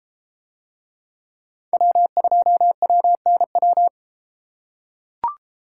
Morse code, heard as a keyed tone, about 31 wpm, 700 Hz. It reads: W2WNW